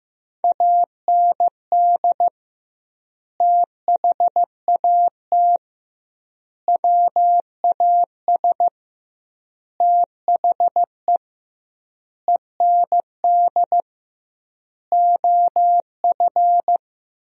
Morse code, heard as a keyed tone, about 16 wpm, 700 Hz. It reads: AND THAT WAS THE END OF
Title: 2x Word-Spacing - Sentences from Top 200 Words - 15wpm